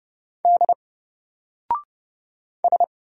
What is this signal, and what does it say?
Morse code, heard as a keyed tone, about 30 wpm, 700 Hz. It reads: D S
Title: Single Character (Letter:Number:Punctuation:Prosign) 30wpm